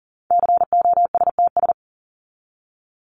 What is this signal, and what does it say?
Morse code, heard as a keyed tone, about 40 wpm, 700 Hz. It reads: COSTS
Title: ES:S Suffix - Top 100 Words with Base Words 40wpm